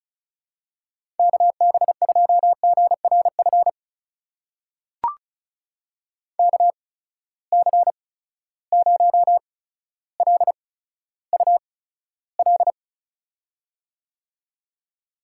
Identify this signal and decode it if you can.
Morse code, heard as a keyed tone, about 34 wpm, spaced as for 15 wpm, 700 Hz. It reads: KB2GRF KC0LUL
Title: USA Call Signs - 6 Characters - 8x Character Spacing - 35wpm